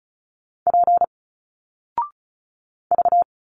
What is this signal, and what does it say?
Morse code, heard as a keyed tone, about 36 wpm, 700 Hz. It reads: P V
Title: Single Letters 35wpm